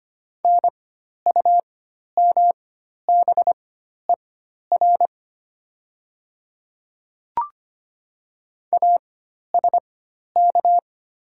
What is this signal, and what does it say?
Morse code, heard as a keyed tone, about 25 wpm, spaced as for 12 wpm, 690 Hz. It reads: NUMBER ASK
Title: 4x Character Spacing - Top 300 Words - 25wpm